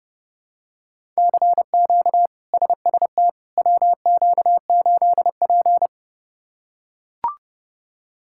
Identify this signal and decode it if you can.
Morse code, heard as a keyed tone, about 30 wpm, 700 Hz. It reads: CQ SST WQ8P
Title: K1USN SST QSO Practice - 30wpm